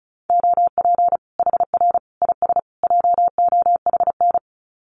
Morse code, heard as a keyed tone, about 34 wpm, 700 Hz. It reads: OP HR IS JOHN
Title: Common QSO Phrases 35wpm